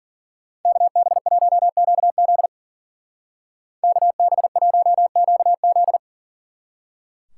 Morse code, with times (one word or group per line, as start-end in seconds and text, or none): 0.7-2.5 KB1QZ
3.8-6.0 KB1QZ